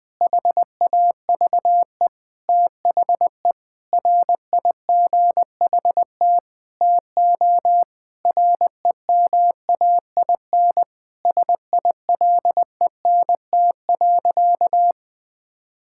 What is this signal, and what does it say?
Morse code, heard as a keyed tone, about 21 wpm, 700 Hz. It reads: HAVE THE RIGHT TO REMAIN SILENT.
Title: Sentences with 4-Word Phrases - 20wpm